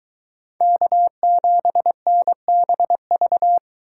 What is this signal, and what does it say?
Morse code, heard as a keyed tone, about 23 wpm, 700 Hz. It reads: K7NBV